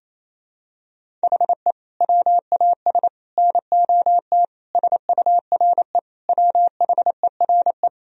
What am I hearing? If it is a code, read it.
Morse code, heard as a keyed tone, about 28 wpm, 700 Hz. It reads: HE WAS NOT SURE WHERE
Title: Sentences from Top 1000 Words - Review Part 1 - 28wpm